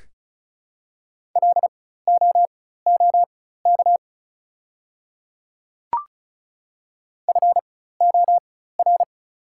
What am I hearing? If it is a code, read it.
Morse code, heard as a keyed tone, about 36 wpm, spaced as for 16 wpm, 700 Hz. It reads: LOOK FOR
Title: 4x Character Spacing - Top 100 Words - 35wpm